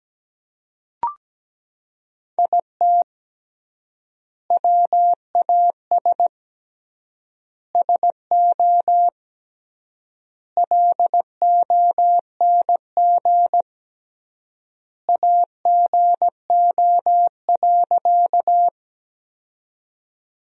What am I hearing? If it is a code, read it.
Morse code, heard as a keyed tone, about 17 wpm, 700 Hz. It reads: IT WAS SO LONG AGO.
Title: 3x Word-Spacing - Sentences from Top 500 Words - 17wpm